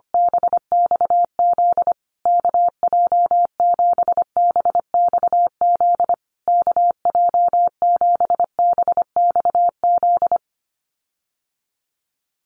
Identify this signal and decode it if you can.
Morse code, heard as a keyed tone, about 25 wpm, 700 Hz. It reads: BXZ KJ7BXZ KJ7BXZ
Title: USA Call Signs Repeated 3x 25wpm